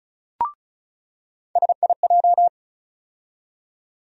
Morse code, heard as a keyed tone, about 34 wpm, 700 Hz. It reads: SIJ
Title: USA Call Sign Postfixes - 3 Letters 35wpm